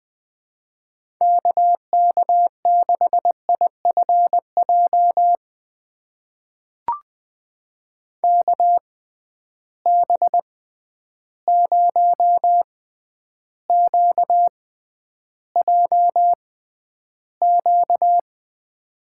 Morse code, heard as a keyed tone, about 21 wpm, spaced as for 10 wpm, 700 Hz. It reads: KK6IFJ KB0QJQ
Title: USA Call Signs - 6x Character Spacing - 20wpm